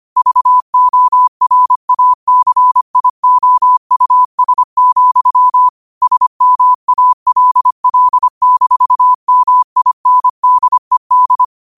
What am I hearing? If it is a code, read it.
Morse code, heard as a keyed tone, about 25 wpm, 1000 Hz. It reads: VORACIOUS, SMALL-MINDED